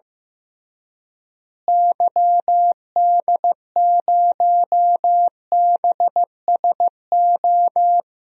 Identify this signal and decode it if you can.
Morse code, heard as a keyed tone, about 16 wpm, 700 Hz. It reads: YD0BSO
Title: DXCC-100 Call Signs - 6 Characters - Repeated 3x - 15wpm